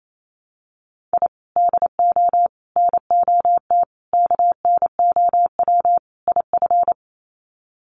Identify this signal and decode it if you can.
Morse code, heard as a keyed tone, about 28 wpm, 700 Hz. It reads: I DO NOT KNOW IF
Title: Sets of 5 Words 28wpm